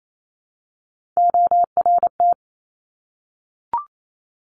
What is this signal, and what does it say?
Morse code, heard as a keyed tone, about 28 wpm, 700 Hz. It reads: ORT
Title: Top Three Letter Combinations in Words 28wpm